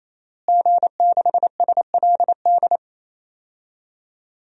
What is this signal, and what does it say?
Morse code, heard as a keyed tone, about 27 wpm, 700 Hz. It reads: G6SLD
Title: DXCC-100 Call Signs - 5 Characters - Repeated 3x - 28wpm